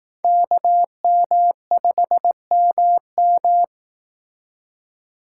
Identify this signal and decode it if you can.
Morse code, heard as a keyed tone, about 18 wpm, 700 Hz. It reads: KM5MM
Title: Speed Racing - USA Call Signs 15wpm